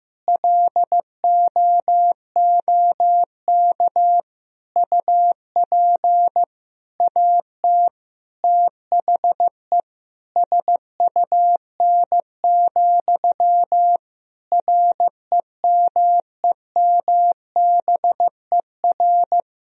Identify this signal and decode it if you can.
Morse code, heard as a keyed tone, about 15 wpm, 700 Hz. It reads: LOOK UP AT THE SUN, REMEMBER
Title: Sentences from Top 1000 Words - Review Part 1 - 15wpm